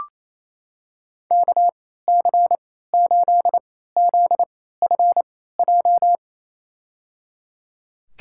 Morse code, heard as a keyed tone, about 28 wpm, spaced as for 16 wpm, 700 Hz. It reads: KC8ZFJ
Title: USA Call Signs - 6 Characters - 3x Character Spacing - 28wpm